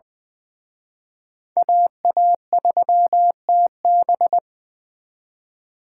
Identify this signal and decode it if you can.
Morse code, heard as a keyed tone, about 23 wpm, 700 Hz. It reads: AA3TB
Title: Speed Racing - USA 2x2 Call Signs 20wpm